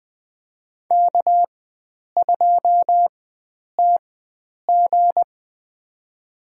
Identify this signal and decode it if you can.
Morse code, heard as a keyed tone, about 20 wpm, spaced as for 9 wpm, 700 Hz. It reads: K2TG
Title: USA Call Signs - 4 Characters - 4x Character Spacing - 20wpm